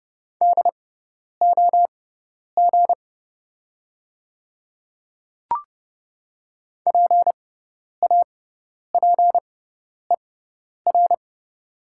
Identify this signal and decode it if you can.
Morse code, heard as a keyed tone, about 30 wpm, spaced as for 10 wpm, 690 Hz. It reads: DOG PAPER
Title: 6x Character Spacing - Top 500 Words - 30wpm